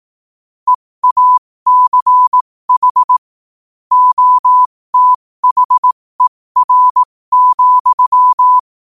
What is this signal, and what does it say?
Morse code, heard as a keyed tone, about 18 wpm, spaced as for 15 wpm, 1000 Hz. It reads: EACH OTHER,
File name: qotdfiles / 15wpm